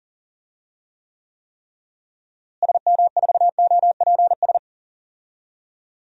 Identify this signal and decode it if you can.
Morse code, heard as a keyed tone, about 40 wpm, 700 Hz. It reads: SM4OPS
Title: DXCC-100 Call Signs - 6 Characters - Repeated 3x - 40wpm